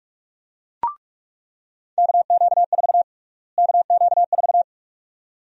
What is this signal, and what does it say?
Morse code, heard as a keyed tone, about 44 wpm, 700 Hz. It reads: KQ4 KQ4
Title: USA Call Sign Prefixes - Repeated 2x - 45wpm